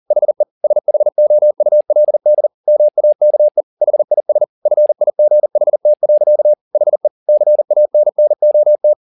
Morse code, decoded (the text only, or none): HE SHOULD MAKE HIS FIGHT. HE CANNOT